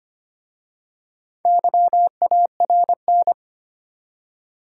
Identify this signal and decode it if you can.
Morse code, heard as a keyed tone, about 25 wpm, 700 Hz. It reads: YARN
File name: practice / 25wpm / Easy Words with Hints - 25wpm